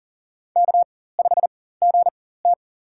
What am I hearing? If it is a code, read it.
Morse code, heard as a keyed tone, about 42 wpm, spaced as for 19 wpm, 700 Hz. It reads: K5GT
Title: USA Call Signs - 4 Characters - 4x Character Spacing - 40wpm